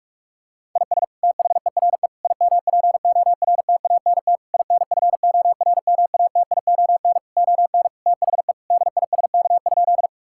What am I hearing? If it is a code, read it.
Morse code, heard as a keyed tone, about 41 wpm, 690 Hz. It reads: IS THERE IMPORTANT INFORMATION ON THE DISK?
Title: Sentences from Top 1000 Words - Review Part 6 - 45wpm